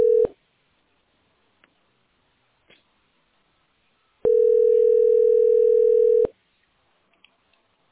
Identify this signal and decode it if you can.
Ringback tone (North America), 440 + 480 Hz.